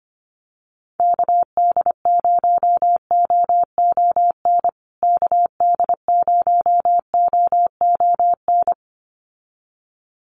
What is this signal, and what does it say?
Morse code, heard as a keyed tone, about 25 wpm, 700 Hz. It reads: KD0OON KD0OON
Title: USA Call Signs Repeated 2x 25wpm